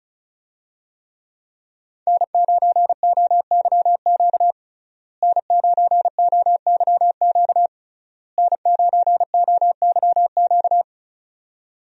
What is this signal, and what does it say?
Morse code, heard as a keyed tone, about 34 wpm, 700 Hz. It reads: N9OYQ N9OYQ N9OYQ
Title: USA Call Signs - 5 Characters - Repeated 3x - 35wpm